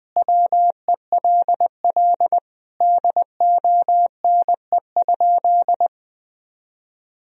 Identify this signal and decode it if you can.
Morse code, heard as a keyed tone, about 21 wpm, 700 Hz. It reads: WELL DONE?
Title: Sentences from Top 1000 Words - Review Part 3 - 20wpm